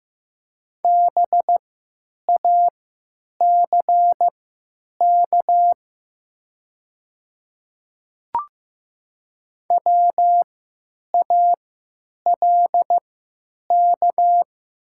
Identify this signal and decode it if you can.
Morse code, heard as a keyed tone, about 15 wpm, spaced as for 9 wpm, 700 Hz. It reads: BACK WALK
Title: 3x Character Spacing - Top 300 Words - 15wpm